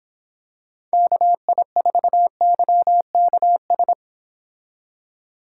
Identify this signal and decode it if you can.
Morse code, heard as a keyed tone, about 26 wpm, 700 Hz. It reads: KI4YKS